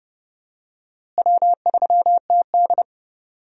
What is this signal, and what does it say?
Morse code, heard as a keyed tone, about 30 wpm, 700 Hz. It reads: W3TD
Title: USA Call Signs - 4 Characters - Repeated 3x - 30wpm